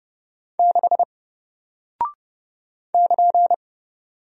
Morse code, heard as a keyed tone, about 30 wpm, 700 Hz. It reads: 6 <KN>
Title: Single Character (Letter:Number:Punctuation:Prosign) Alt 30wpm